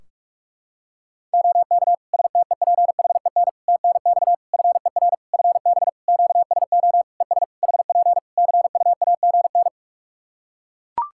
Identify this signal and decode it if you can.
Morse code, heard as a keyed tone, about 42 wpm, 700 Hz. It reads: OK STEPHEN TNX FER FB QSO ES HP CUAGN
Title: K1USN SST QSO Practice 45wpm